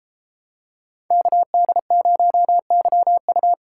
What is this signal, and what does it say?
Morse code, heard as a keyed tone, about 32 wpm, 700 Hz. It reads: KD0YU